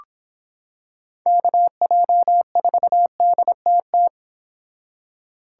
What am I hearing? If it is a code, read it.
Morse code, heard as a keyed tone, about 26 wpm, 700 Hz. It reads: KJ4DTT